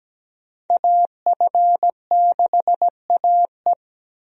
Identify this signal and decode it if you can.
Morse code, heard as a keyed tone, about 17 wpm, 700 Hz. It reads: AF6AE
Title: USA Call Signs - 5 Characters - Repeated 3x - 17wpm